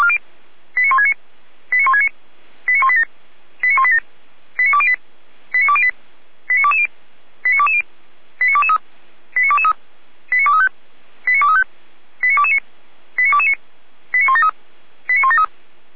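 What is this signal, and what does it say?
Five-tone selective call (ZVEI), heard as quick runs of five tones, tones of about 70 ms. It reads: 89169 89169 89187 89187 89208 89208 89220 89220 89223 89223 89235 89235 89229 89229 89173 89173
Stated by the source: ZVEI five-tone selective call received by SDR.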